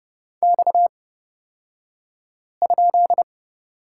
Morse code, heard as a keyed tone, about 30 wpm, spaced as for 5 wpm, 700 Hz. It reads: X?
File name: Lesson 38d - Letter q - Two Characters 30wpm